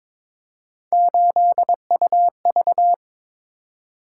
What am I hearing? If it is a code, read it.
Morse code, heard as a keyed tone, about 22 wpm, 700 Hz. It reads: 8UV
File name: practice / 22wpm / Three Letter-Number Combinations - 22wpm